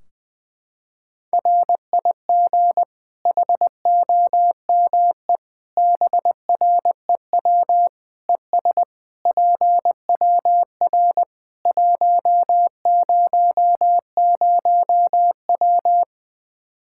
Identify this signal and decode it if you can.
Morse code, heard as a keyed tone, about 21 wpm, 700 Hz. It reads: RIG HOME BREW ES PWR 100W
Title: Practice QSO - Exchange 2 - Info - 20wpm